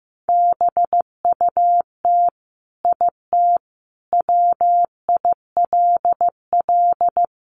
Morse code, heard as a keyed tone, about 15 wpm, 700 Hz. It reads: BUT IT WILL
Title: Sentences from Top 1300 Words 15wpm